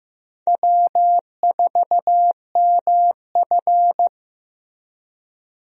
Morse code, heard as a keyed tone, about 15 wpm, 700 Hz. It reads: W4MF
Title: USA Call Signs - 4 Characters - 15wpm